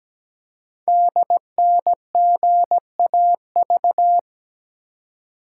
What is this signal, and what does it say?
Morse code, heard as a keyed tone, about 17 wpm, 700 Hz. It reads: DNGAV
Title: Code Groups - 5 Letters - 17wpm